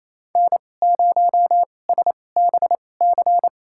Morse code, heard as a keyed tone, about 28 wpm, spaced as for 20 wpm, 700 Hz. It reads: N0SBC
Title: USA Call Signs - 5 Characters - 2x Character Spacing - 28wpm